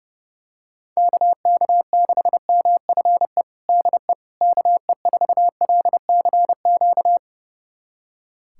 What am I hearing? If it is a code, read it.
Morse code, heard as a keyed tone, about 32 wpm, 700 Hz. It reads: KK6MFE DE KE4LCQ
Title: USA Callsign DE Callsign - 6 Characters - 30wpm